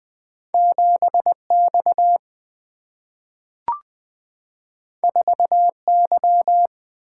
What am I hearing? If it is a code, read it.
Morse code, heard as a keyed tone, about 20 wpm, 700 Hz. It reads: 7X 4Y
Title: Single-Letter Single-Number 20wpm